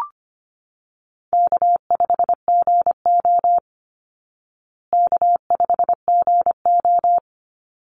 Morse code, heard as a keyed tone, about 25 wpm, 700 Hz. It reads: K5GO K5GO
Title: Call Signs Repeated 3x - 25wpm